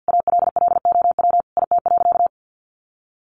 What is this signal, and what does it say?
Morse code, heard as a keyed tone, about 50 wpm, 700 Hz. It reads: ALLOW IT.
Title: Sentences from Top 700 Words 50wpm